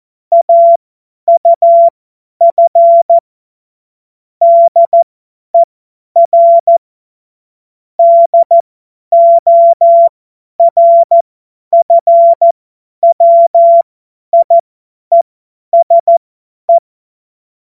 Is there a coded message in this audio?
Yes: Morse code from 0.3 to 16.8 s, heard as a keyed tone, about 14 wpm, spaced as for 10 wpm, 680 Hz. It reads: AUF DER DORFWIESE